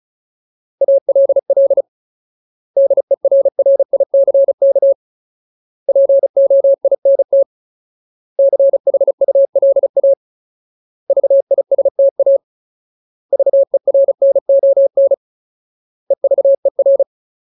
Morse code, heard as a keyed tone, about 34 wpm, 550 Hz. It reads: ALL DERRICK POINT CHULA VISTA VERNON EVER
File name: Ultimate Word List / 35WPM Ultimate